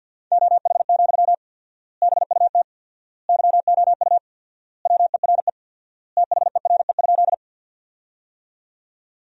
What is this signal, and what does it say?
Morse code, heard as a keyed tone, about 53 wpm, 700 Hz. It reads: OH, BUT YOU WERE THERE?
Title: 4x Word-Spacing - Sentences from Top 500 Words - 50wpm